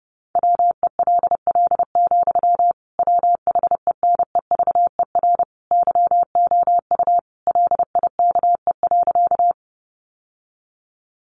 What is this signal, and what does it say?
Morse code, heard as a keyed tone, about 30 wpm, 700 Hz. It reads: WELL, WHENEVER YOU LIKE.
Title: Sentences from Top 1200 Words 30wpm